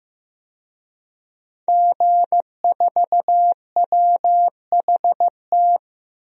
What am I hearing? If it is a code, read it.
Morse code, heard as a keyed tone, about 15 wpm, 700 Hz. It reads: G4WHT
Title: DXCC-100 Call Signs - 5 Characters - 15wpm